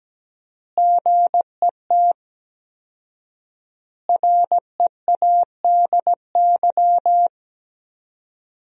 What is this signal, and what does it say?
Morse code, heard as a keyed tone, about 17 wpm, 700 Hz. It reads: GET READY
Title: 4x Word-Spacing - Sentences from Top 400 Words - 17wpm